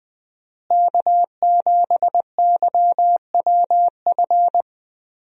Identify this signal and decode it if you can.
Morse code, heard as a keyed tone, about 20 wpm, 700 Hz. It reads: K7YWF
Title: USA Call Signs - 5 Characters - Repeated 2x - 20wpm